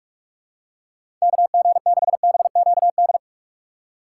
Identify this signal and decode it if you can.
Morse code, heard as a keyed tone, about 47 wpm, 700 Hz. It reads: KG6BQD
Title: USA Call Signs - 6 Characters - 6x Character Spacing - 45wpm